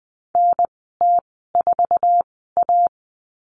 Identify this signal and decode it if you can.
Morse code, heard as a keyed tone, about 20 wpm, spaced as for 14 wpm, 700 Hz. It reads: NT4A
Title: 062 Lesson 11g - Letter h - Call Signs 20wpm